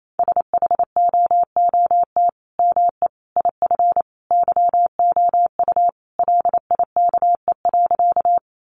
Morse code, heard as a keyed tone, about 28 wpm, 700 Hz. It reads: SHOOT ME IF YOU LIKE.
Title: Sentences from Top 1200 Words 28wpm